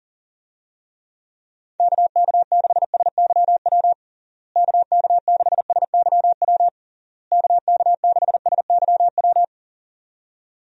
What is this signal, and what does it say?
Morse code, heard as a keyed tone, about 40 wpm, 700 Hz. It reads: KK6SYW KK6SYW KK6SYW
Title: USA Call Signs - 6 Characters - Repeated 3x - 40wpm